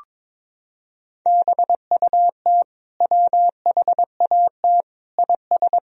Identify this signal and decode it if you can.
Morse code, heard as a keyed tone, about 22 wpm, 700 Hz. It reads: BUT WHAT IS
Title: Sentences from Top 300 Words and Abbreviations 22wpm